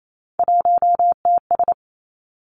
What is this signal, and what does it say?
Morse code, heard as a keyed tone, about 28 wpm, 700 Hz. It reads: 1TS
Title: Three Challenging Letter-Number Combinations 28wpm